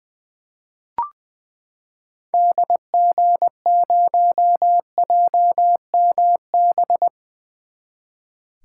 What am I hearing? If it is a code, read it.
Morse code, heard as a keyed tone, about 21 wpm, 700 Hz. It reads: DG0JMB
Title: 2023 SOTA Activator Call Signs - 20wpm